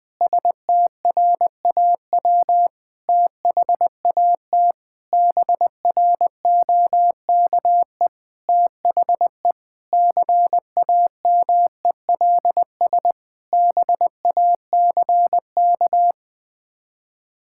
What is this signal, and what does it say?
Morse code, heard as a keyed tone, about 20 wpm, 700 Hz. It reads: STRAW THAT BROKE THE CAMELS BACK
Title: Sets of 8 Words - 20wpm